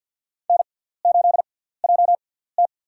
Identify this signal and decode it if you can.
Morse code, heard as a keyed tone, about 51 wpm, spaced as for 17 wpm, 700 Hz. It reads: N8JT
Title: USA Call Signs - 4 Characters - 6x Character Spacing - 50wpm